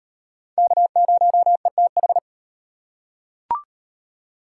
Morse code, heard as a keyed tone, about 33 wpm, 700 Hz. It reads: K0ETH